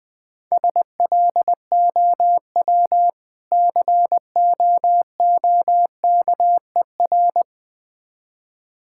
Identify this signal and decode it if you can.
Morse code, heard as a keyed tone, about 20 wpm, 700 Hz. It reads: SLOW COOKER
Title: Difficult Words - Speed-Racing with Hints - 20wpm